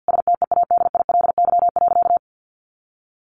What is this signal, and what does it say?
Morse code, heard as a keyed tone, about 49 wpm, 700 Hz. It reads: STEADILY.
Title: A Sound Of Thunder 50wpm